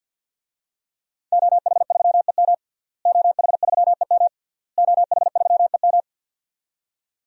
Morse code, heard as a keyed tone, about 47 wpm, 700 Hz. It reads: OH3EM OH3EM OH3EM
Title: DXCC-100 Call Signs - 5 Characters - Repeated 3x - 50wpm